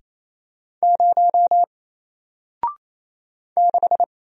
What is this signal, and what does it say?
Morse code, heard as a keyed tone, about 27 wpm, 700 Hz. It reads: 0 6
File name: Single-Letter-Number-28wpm